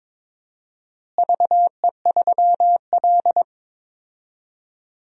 Morse code, heard as a keyed tone, about 22 wpm, 700 Hz. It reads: VE3L
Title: DXCC-100 Call Signs - 4 Characters - Repeated 3x - 22wpm